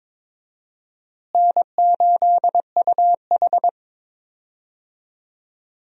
Morse code, heard as a keyed tone, about 22 wpm, 700 Hz. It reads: N8UH
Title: USA Call Signs - 4 Characters - Repeated 2x - 22wpm